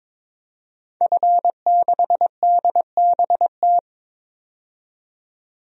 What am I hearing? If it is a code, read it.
Morse code, heard as a keyed tone, about 22 wpm, 700 Hz. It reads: F6DBT
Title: DXCC-100 Call Signs - 5 Characters - 22wpm